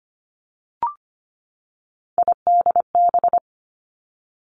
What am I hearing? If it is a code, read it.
Morse code, heard as a keyed tone, about 25 wpm, 700 Hz. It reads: IDB
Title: Three Challenging Letter-Number Combinations 25wpm